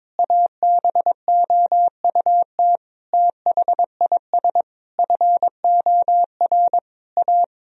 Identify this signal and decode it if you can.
Morse code, heard as a keyed tone, about 22 wpm, 700 Hz. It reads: ABOUT THIS FOR A
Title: Sentences from Top 1000 Words - Review Part 1 - 22wpm